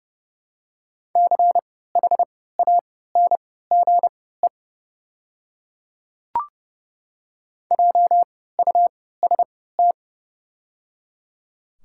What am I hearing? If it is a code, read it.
Morse code, heard as a keyed tone, about 31 wpm, spaced as for 17 wpm, 690 Hz. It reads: CHANGE JUST
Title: 3x Character Spacing - Top 200 Words - 30wpm